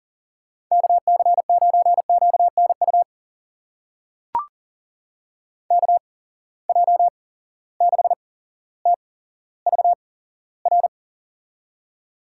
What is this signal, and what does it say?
Morse code, heard as a keyed tone, about 41 wpm, spaced as for 17 wpm, 700 Hz. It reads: KC9QNU KJ6TVR
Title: USA Call Signs - 6 Characters - 8x Character Spacing - 40wpm